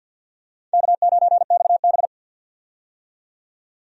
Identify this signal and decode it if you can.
Morse code, heard as a keyed tone, about 51 wpm, 700 Hz. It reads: K9XB
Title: USA Call Signs - 4 Characters - 50wpm